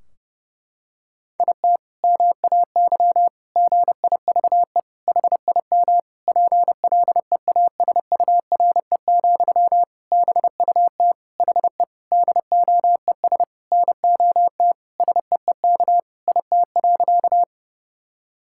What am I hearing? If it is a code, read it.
Morse code, heard as a keyed tone, about 30 wpm, 700 Hz. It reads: IT MAY GIVE HIM PLEASURE, BUT HE DOES NOT SEEK IT.
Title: Sentences from Top 1600 Words 30wpm